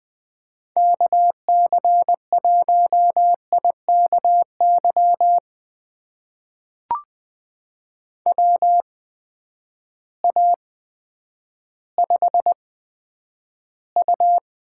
Morse code, heard as a keyed tone, about 20 wpm, spaced as for 10 wpm, 700 Hz. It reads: KC1IKY WA5U
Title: USA Call Signs - 6 Characters - 8x Character Spacing - 20wpm